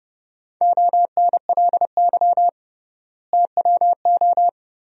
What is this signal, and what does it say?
Morse code, heard as a keyed tone, about 29 wpm, 700 Hz. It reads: ONLY TWO
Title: 3x Word-Spacing - Sentences from Top 200 Words - 30wpm